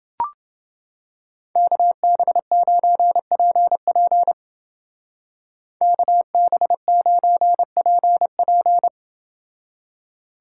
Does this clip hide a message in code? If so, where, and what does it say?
Yes: Morse code from 1.6 to 8.9 s, heard as a keyed tone, about 30 wpm, 700 Hz. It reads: KB9PP KB9PP